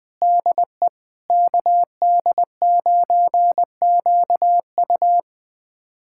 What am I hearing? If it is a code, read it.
Morse code, heard as a keyed tone, about 20 wpm, 700 Hz. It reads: DE KD9QU
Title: USA Callsign DE Callsign 20wpm